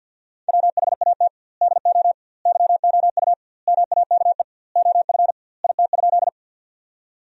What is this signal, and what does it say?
Morse code, heard as a keyed tone, about 47 wpm, 700 Hz. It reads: WHAT DO YOU MAKE OF IT?
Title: 2x Word-Spacing - Sentences from Top 100 Words - 50wpm